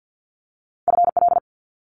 Morse code, heard as a keyed tone, about 50 wpm, 700 Hz. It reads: FL
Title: Multi-letter Phonemes Quick Fire 50wpm